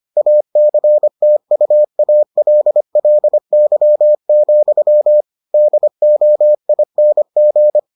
Morse code, heard as a keyed tone, about 25 wpm, 600 Hz. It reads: ACTUALLY, DOING